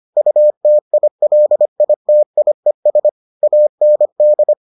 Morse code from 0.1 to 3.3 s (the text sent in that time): UTILITIES